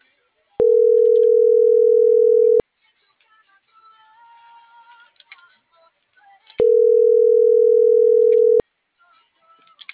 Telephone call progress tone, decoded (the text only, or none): ringback tone